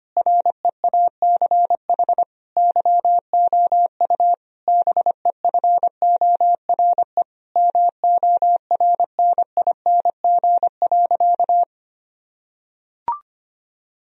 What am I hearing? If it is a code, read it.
Morse code, heard as a keyed tone, about 25 wpm, 700 Hz. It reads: REACH YOU BEFORE MORNING.
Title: Sentences from Top 1000 Words - Review Part 2 - 25wpm